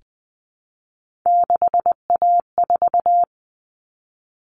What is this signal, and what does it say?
Morse code, heard as a keyed tone, about 20 wpm, 700 Hz. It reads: 6A4
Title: All Cut Numbers - 3-Digit 20wpm